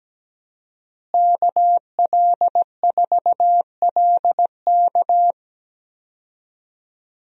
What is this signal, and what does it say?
Morse code, heard as a keyed tone, about 17 wpm, 700 Hz. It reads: KL4LK
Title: USA Call Signs - 5 Characters - Repeated 3x - 17wpm